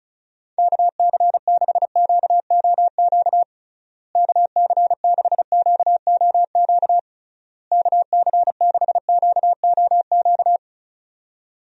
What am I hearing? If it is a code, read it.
Morse code, heard as a keyed tone, about 36 wpm, 700 Hz. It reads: KC6QOQ KC6QOQ KC6QOQ
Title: USA Call Signs - 6 Characters - Repeated 3x - 35wpm